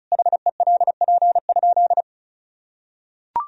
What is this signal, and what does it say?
Morse code, heard as a keyed tone, about 36 wpm, 700 Hz. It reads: HELP?
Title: Sentences from Top 200 Words and Abbreviations 35wpm